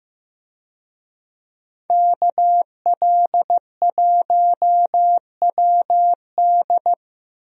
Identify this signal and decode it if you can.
Morse code, heard as a keyed tone, about 15 wpm, 700 Hz. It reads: KL1WD
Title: USA Call Signs - 5 Characters - Repeated 2x - 15wpm